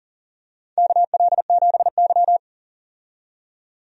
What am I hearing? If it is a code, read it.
Morse code, heard as a keyed tone, about 40 wpm, 700 Hz. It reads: KL7Y
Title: DXCC-100 Call Signs - 4 Characters - 40wpm